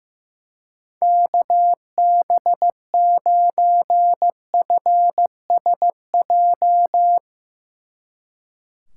Morse code, heard as a keyed tone, about 15 wpm, 700 Hz. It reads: KB9FSJ
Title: Speed Racing - USA 2x3 Call Signs 15wpm